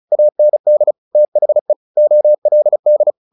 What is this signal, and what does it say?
Morse code, heard as a keyed tone, about 35 wpm, 600 Hz. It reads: AND THE OLD
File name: the old man and the sea 2 35wpm